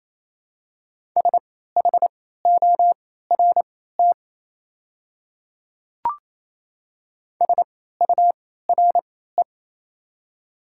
Morse code, heard as a keyed tone, about 27 wpm, spaced as for 16 wpm, 700 Hz. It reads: SHORT SURE
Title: 3x Character Spacing - Top 500 Words - 28wpm